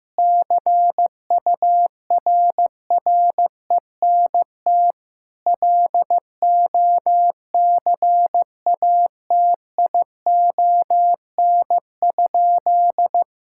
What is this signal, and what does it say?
Morse code, heard as a keyed tone, about 15 wpm, 700 Hz. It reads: CURRENT LOCATION?
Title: Sentences from Top 1000 Words - Review Part 6 - 15wpm